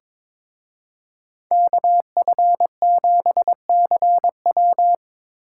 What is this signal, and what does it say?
Morse code, heard as a keyed tone, about 25 wpm, 700 Hz. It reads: KF7CW
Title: Speed Racing - USA 2x2 Call Signs 22wpm